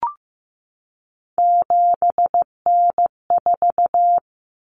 Morse code, heard as a keyed tone, about 15 wpm, 700 Hz. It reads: 7N4
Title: Common Cut Numbers - 3-Digit 15wpm